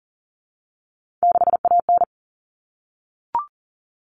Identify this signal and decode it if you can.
Morse code, heard as a keyed tone, about 40 wpm, 700 Hz. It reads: 6AN
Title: All Cut Numbers - 3-Digit 40wpm